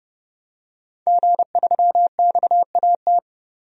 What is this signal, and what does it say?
Morse code, heard as a keyed tone, about 30 wpm, 700 Hz. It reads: G3XAT
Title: DXCC-100 Call Signs - 5 Characters - 30wpm